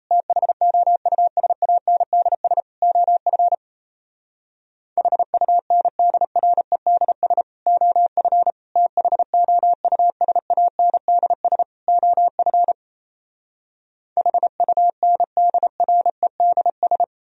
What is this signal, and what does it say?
Morse code, heard as a keyed tone, about 38 wpm, 700 Hz. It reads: THOUSANDS OF HUNDREDS OF THOUSANDS OF HUNDREDS